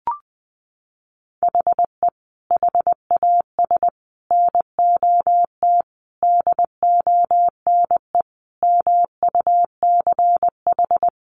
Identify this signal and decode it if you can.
Morse code, heard as a keyed tone, about 20 wpm, 700 Hz. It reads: HE HAS NOT DONE MUCH
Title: Sentences from Top 1100 Words 20wpm